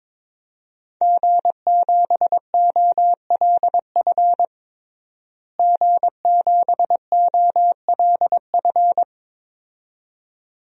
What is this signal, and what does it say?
Morse code, heard as a keyed tone, about 22 wpm, 700 Hz. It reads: G7OLF G7OLF
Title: DXCC-100 Call Signs - 5 Characters - Repeated 2x - 22wpm